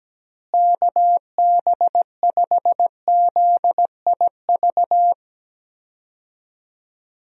Morse code, heard as a keyed tone, about 17 wpm, 700 Hz. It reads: KB5ZIV
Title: USA Call Signs - 6 Characters - Repeated 3x - 17wpm